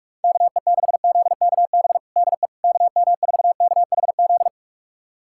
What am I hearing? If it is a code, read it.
Morse code, heard as a keyed tone, about 46 wpm, 700 Hz. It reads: KE6ZKB DE KM4KHZ
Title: USA Callsign DE Callsign 45wpm